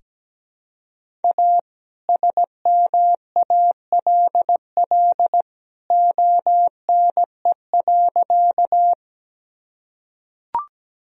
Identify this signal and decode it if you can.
Morse code, heard as a keyed tone, about 17 wpm, 700 Hz. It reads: A SMALL ONE.
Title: Sentences from Top 1000 Words - Review Part 1 - 17wpm